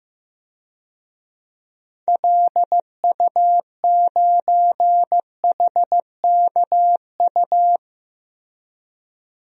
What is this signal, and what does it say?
Morse code, heard as a keyed tone, about 15 wpm, 700 Hz. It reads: LU9HKU
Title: DXCC-100 Call Signs - 6 Characters - Repeated 3x - 15wpm